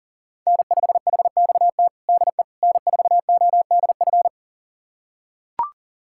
Morse code, heard as a keyed tone, about 40 wpm, 700 Hz. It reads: N5HXT DE N4ODF
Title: USA Callsign DE Callsign - 5 Characters - 40wpm